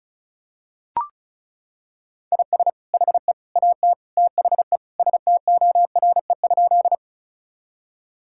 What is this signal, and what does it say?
Morse code, heard as a keyed tone, about 36 wpm, 700 Hz. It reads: IS HE AT THE STORE?
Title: Sentences from Top 1000 Words - Review Part 5 - 35wpm